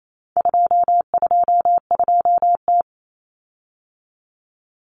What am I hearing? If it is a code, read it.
Morse code, heard as a keyed tone, about 28 wpm, 700 Hz. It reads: 222T
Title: Common Cut Numbers - 4-Digit 28wpm